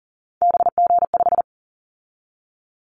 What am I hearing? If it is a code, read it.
Morse code, heard as a keyed tone, about 42 wpm, 700 Hz. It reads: BG5
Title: Callsign Prefixes for Top 100 DXCC - 40wpm